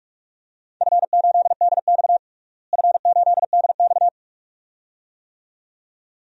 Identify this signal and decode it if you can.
Morse code, heard as a keyed tone, about 44 wpm, 700 Hz. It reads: F8DX F8DX
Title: DXCC-100 Call Signs - 4 Characters - Repeated 2x - 45wpm